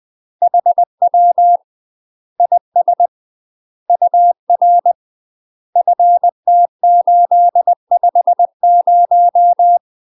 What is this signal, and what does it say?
Morse code, heard as a keyed tone, about 20 wpm, 700 Hz. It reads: HW IS UR FT850
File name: UK MASTER QSOs TRAINING / 20WPM / UK MASTER QSOs 0053